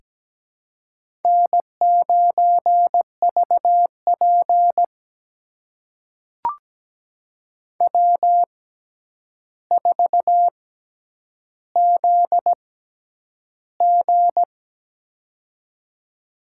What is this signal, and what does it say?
Morse code, heard as a keyed tone, about 17 wpm, spaced as for 9 wpm, 700 Hz. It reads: N9VP W4ZG
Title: USA Call Signs - 4 Characters - 6x Character Spacing - 17wpm